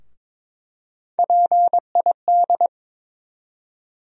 Morse code, heard as a keyed tone, about 22 wpm, 700 Hz. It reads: PID
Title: USA Call Sign Postfixes - 3 Letters 22wpm